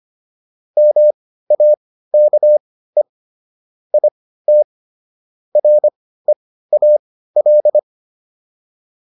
Morse code, heard as a keyed tone, about 25 wpm, spaced as for 15 wpm, 600 Hz. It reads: MAKE IT REAL
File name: PR305 15